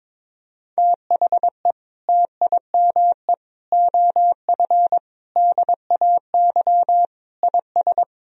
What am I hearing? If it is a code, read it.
Morse code, heard as a keyed tone, about 22 wpm, 700 Hz. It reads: THE TIME OF DAY IS
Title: Sentences from Top 1000 Words - Review Part 3 - 22wpm